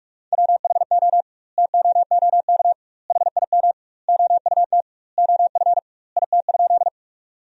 Morse code, heard as a keyed tone, about 41 wpm, 700 Hz. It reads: WHO TOOK HIM OUT OF IT?
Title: 2x Word-Spacing - Sentences from Top 300 Words - 45wpm